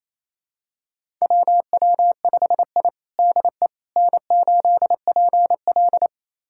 Morse code, heard as a keyed tone, about 28 wpm, 700 Hz. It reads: WW5I DE N8PL
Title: USA Callsign DE Callsign - 4 Characters - 28wpm